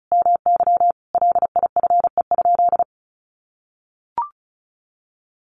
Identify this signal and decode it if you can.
Morse code, heard as a keyed tone, about 34 wpm, 700 Hz. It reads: MY LIFE?
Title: Sentences from Top 300 Words 35wpm